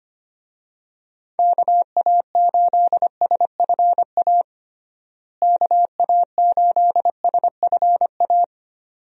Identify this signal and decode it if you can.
Morse code, heard as a keyed tone, about 25 wpm, 700 Hz. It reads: KA8SFA KA8SFA
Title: USA Call Signs - 6 Characters - Repeated 2x - 25wpm